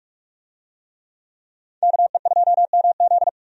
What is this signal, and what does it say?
Morse code, heard as a keyed tone, about 46 wpm, 700 Hz. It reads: KE2MZ
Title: USA Call Signs - 5 Characters - 45wpm